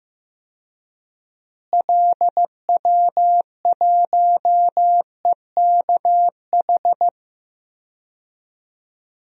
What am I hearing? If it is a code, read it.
Morse code, heard as a keyed tone, about 15 wpm, 700 Hz. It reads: LW1EKH
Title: DXCC-100 Call Signs - 6 Characters - Repeated 2x - 15wpm